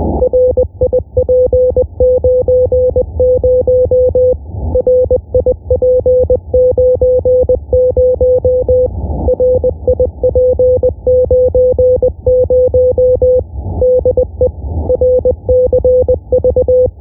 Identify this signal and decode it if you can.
Morse code, heard as a keyed tone, about 20 wpm, 500 Hz. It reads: RIP90 RIP90 RIP90 DE RCV